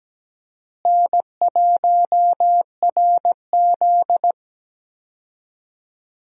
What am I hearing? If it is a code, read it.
Morse code, heard as a keyed tone, about 17 wpm, 700 Hz. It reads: N1RZ
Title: USA Call Signs - 4 Characters - Repeated 3x - 17wpm